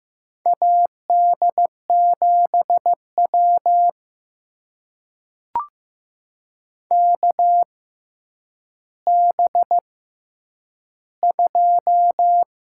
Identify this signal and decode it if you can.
Morse code, heard as a keyed tone, about 15 wpm, spaced as for 9 wpm, 700 Hz. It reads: AD7W KB2
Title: USA Call Signs - 6x Character Spacing - 15wpm